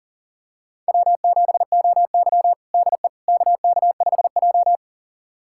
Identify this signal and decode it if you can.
Morse code, heard as a keyed tone, about 40 wpm, 700 Hz. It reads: W7OY DE KK5J
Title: USA Callsign DE Callsign - 4 Characters - 40wpm